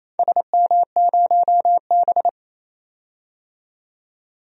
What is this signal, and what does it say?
Morse code, heard as a keyed tone, about 28 wpm, 700 Hz. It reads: SM0B
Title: DXCC-100 Call Signs - 4 Characters - 28wpm